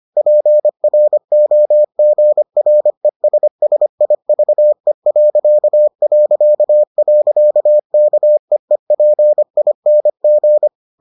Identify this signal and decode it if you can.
Morse code, heard as a keyed tone, about 25 wpm, 600 Hz. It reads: PROGRESSIVE... KEEPING